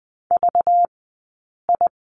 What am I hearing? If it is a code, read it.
Morse code, heard as a keyed tone, about 20 wpm, spaced as for 8 wpm, 700 Hz. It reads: VI
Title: 204 Lesson 33d - Letter j - Two Characters 20wpm